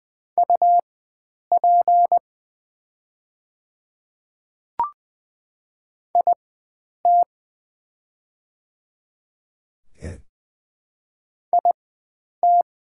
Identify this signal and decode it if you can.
Morse code, heard as a keyed tone, about 20 wpm, spaced as for 9 wpm, 690 Hz. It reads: UP IT IT
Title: 4x Character Spacing - Top 100 Words - 20wpm